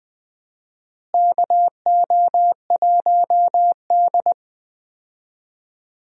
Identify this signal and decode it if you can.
Morse code, heard as a keyed tone, about 20 wpm, 700 Hz. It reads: KO1D
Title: USA Call Signs - 4 Characters - Repeated 3x - 20wpm